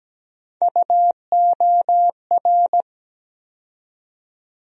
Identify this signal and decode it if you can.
Morse code, heard as a keyed tone, about 17 wpm, 700 Hz. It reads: UOR
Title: USA Call Sign Postfixes - 3 Letters 17wpm